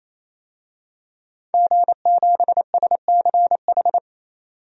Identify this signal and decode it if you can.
Morse code, heard as a keyed tone, about 28 wpm, 700 Hz. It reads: G7SCH
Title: DXCC-100 Call Signs - 5 Characters - 28wpm